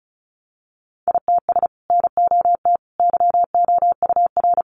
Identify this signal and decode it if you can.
Morse code, heard as a keyed tone, about 34 wpm, 700 Hz. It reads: ITS NOT YOUR
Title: Sentences from Top 600 Words 35wpm